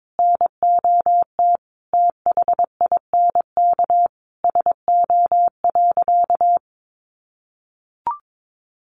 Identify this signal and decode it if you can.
Morse code, heard as a keyed tone, about 22 wpm, 700 Hz. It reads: NOT THINK SO.
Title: Sentences from Top 1500 Words 22wpm